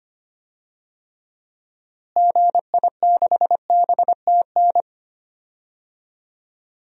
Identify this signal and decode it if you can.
Morse code, heard as a keyed tone, about 25 wpm, 700 Hz. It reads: GI6BTN
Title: DXCC-100 Call Signs - 6 Characters - Repeated 3x - 25wpm